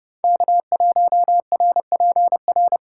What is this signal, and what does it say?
Morse code, heard as a keyed tone, about 28 wpm, 700 Hz. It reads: K1RPR